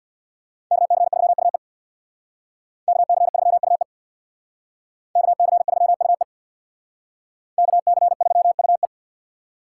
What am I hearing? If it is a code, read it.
Morse code, heard as a keyed tone, about 65 wpm, 700 Hz. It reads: KC3UE KC3UE KC3UE KC3UE